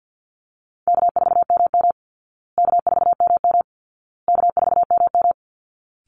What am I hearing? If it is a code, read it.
Morse code, heard as a keyed tone, about 50 wpm, 700 Hz. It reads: K4MM K4MM K4MM
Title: Easy Call Signs Repeated 3x - 50wpm